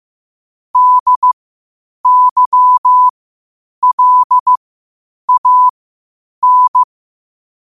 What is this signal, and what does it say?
Morse code, heard as a keyed tone, about 15 wpm, spaced as for 8 wpm, 1000 Hz. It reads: DYLAN
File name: qotdfiles / 8wpm